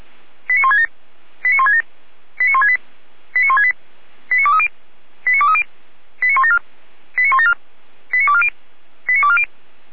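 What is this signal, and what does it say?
Five-tone selective call (ZVEI), heard as quick runs of five tones, tones of about 80 ms. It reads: 89167 89167 89168 89168 89230 89230 89174 89174 89240 89240
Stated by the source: ZVEI five-tone call 89130 received on an SDR.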